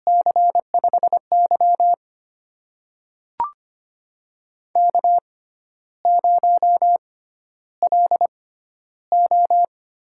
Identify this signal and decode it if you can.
Morse code, heard as a keyed tone, about 25 wpm, spaced as for 12 wpm, 700 Hz. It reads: C5Y K0LO
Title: USA Call Signs - 4 Characters - 6x Character Spacing - 25wpm